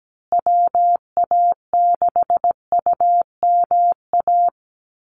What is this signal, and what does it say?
Morse code, heard as a keyed tone, about 17 wpm, 700 Hz. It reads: WA6UMA
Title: USA Call Signs Repeated 3x 17wpm